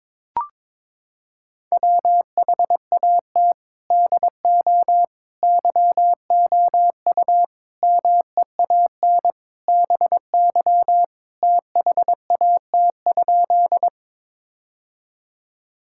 Morse code, heard as a keyed tone, about 22 wpm, 700 Hz. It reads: WHAT DO YOU MEAN BY THAT?
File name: practice / 22wpm / Sentences with 4-Word Phrases - 22wpm